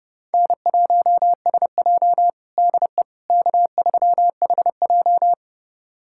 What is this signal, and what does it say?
Morse code, heard as a keyed tone, about 30 wpm, 700 Hz. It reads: N1SJ DE K3HJ
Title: USA Callsign DE Callsign - 4 Characters - 30wpm